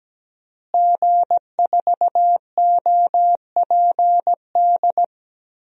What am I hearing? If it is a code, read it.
Morse code, heard as a keyed tone, about 17 wpm, 700 Hz. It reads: G4OPD
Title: DXCC-100 Call Signs - 5 Characters - Repeated 3x - 17wpm